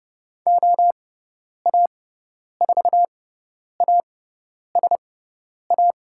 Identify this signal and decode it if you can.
Morse code, heard as a keyed tone, about 30 wpm, spaced as for 10 wpm, 700 Hz. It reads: OA4ASA
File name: Lesson 17g - Letter u - Call Signs 30wpm